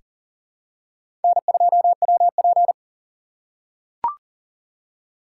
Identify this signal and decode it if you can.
Morse code, heard as a keyed tone, about 40 wpm, 700 Hz. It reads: N2WP
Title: USA Call Signs - 4 Characters - 4x Character Spacing - 40wpm